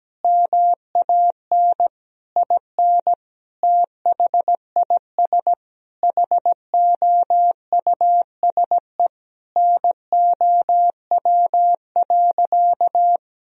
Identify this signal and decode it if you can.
Morse code, heard as a keyed tone, about 17 wpm, 700 Hz. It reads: MAN IN THIS HOUSE NOW.
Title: Sentences from Top 1000 Words - Review Part 2 - 17wpm